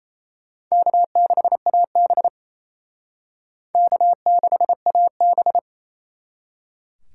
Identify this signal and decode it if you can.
Morse code, heard as a keyed tone, about 37 wpm, 700 Hz. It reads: K6AB K6AB